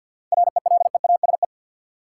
Morse code, heard as a keyed tone, about 53 wpm, 700 Hz. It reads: RELEASE
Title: Sentences from Top 1100 Words - New Word Drill Alt - 50wpm